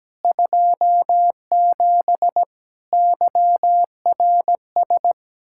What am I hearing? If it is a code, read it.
Morse code, heard as a keyed tone, about 17 wpm, 700 Hz. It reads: 27 YRS
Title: Practice QSO - Exchange 3 - More Info - 17wpm